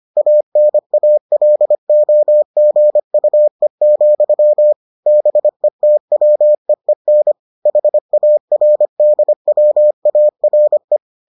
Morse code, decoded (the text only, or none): ANALOGUE, BETWEEN HARDWARE